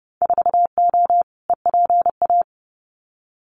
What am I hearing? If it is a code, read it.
Morse code, heard as a keyed tone, about 30 wpm, 700 Hz. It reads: 4O EPA
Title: Winter Field Day Exchanges 30wpm